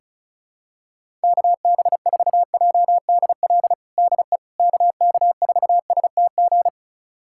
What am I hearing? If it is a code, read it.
Morse code, heard as a keyed tone, about 36 wpm, 700 Hz. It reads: KB4JDL DE KK4STG
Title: USA Callsign DE Callsign 35wpm